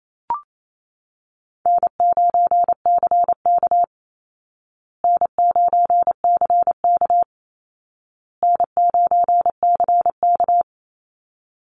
Morse code, heard as a keyed tone, about 28 wpm, 700 Hz. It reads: N9CK N9CK N9CK
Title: Easy Call Signs Repeated 3x - 28wpm